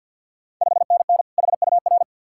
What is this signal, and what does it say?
Morse code, heard as a keyed tone, about 50 wpm, 690 Hz. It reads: 5NN HUR
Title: Ontario QSO Party Exchange 50wpm